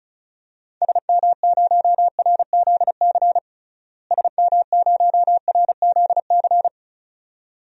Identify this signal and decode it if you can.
Morse code, heard as a keyed tone, about 34 wpm, 700 Hz. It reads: SM0RZC SM0RZC
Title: DXCC-100 Call Signs - 6 Characters - Repeated 2x - 35wpm